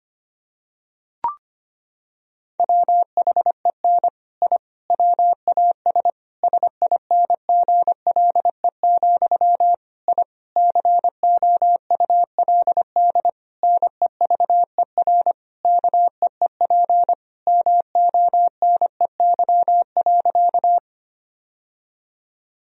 Morse code, heard as a keyed tone, about 25 wpm, 700 Hz. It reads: WHEN I WAS SINGLE, I COULD NEVER KEEP MONEY.
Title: Sentences from Top 1000 Words - Review Part 4 - 25wpm